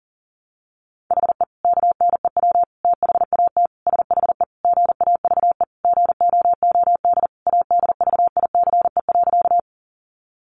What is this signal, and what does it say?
Morse code, heard as a keyed tone, about 40 wpm, 700 Hz. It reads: HE KNEW THAT SHE GAVE GOOD ADVICE.
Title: Sentences from Top 1100 Words 40wpm